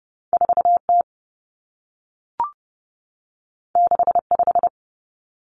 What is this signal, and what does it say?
Morse code, heard as a keyed tone, about 30 wpm, 700 Hz. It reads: 4T 65
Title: Common Cut Numbers - 2-Digit 30wpm